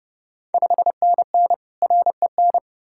Morse code, heard as a keyed tone, about 30 wpm, 700 Hz. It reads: 5NN REN
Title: Ontario QSO Party Exchange 30wpm